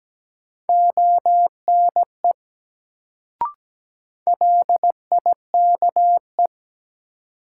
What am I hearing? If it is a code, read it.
Morse code, heard as a keyed tone, about 17 wpm, 700 Hz. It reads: ONE LIKE
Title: Sentences from Top 100 Words - New Word Drill - 17wpm